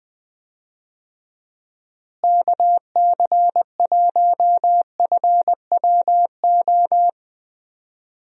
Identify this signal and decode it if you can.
Morse code, heard as a keyed tone, about 21 wpm, 700 Hz. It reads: KC1FWO
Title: USA Call Signs - 6 Characters - Repeated 3x - 20wpm